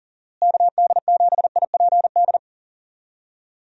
Morse code, heard as a keyed tone, about 40 wpm, 700 Hz. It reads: KD7IPD
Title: USA Call Signs - 6 Characters - Repeated 2x - 40wpm